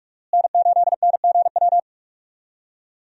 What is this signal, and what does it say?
Morse code, heard as a keyed tone, about 46 wpm, 700 Hz. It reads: N8NGW
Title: USA Call Signs - 5 Characters - Repeated 2x - 45wpm